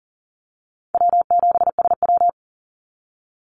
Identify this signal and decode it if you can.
Morse code, heard as a keyed tone, about 40 wpm, 700 Hz. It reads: W7SW
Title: Easy Call Signs Repeated 2x - 40wpm